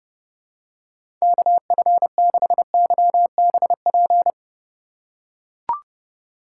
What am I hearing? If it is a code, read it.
Morse code, heard as a keyed tone, about 29 wpm, 700 Hz. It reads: KF6YBP
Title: USA Call Signs - 2x Character Spacing - 30wpm